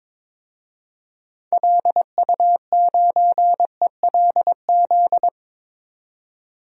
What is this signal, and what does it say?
Morse code, heard as a keyed tone, about 22 wpm, 700 Hz. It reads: LU9ELZ
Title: DXCC-100 Call Signs - 6 Characters - 22wpm